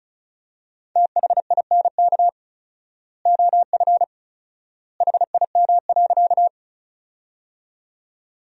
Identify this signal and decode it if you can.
Morse code, heard as a keyed tone, about 36 wpm, 700 Hz. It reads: THINK OF HIM.
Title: 4x Word-Spacing - Sentences from Top 200 Words - 35wpm